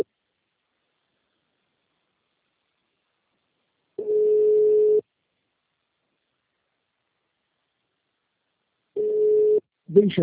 Ringback tone (Europe, 425 Hz).
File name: out- -2016- - - .73371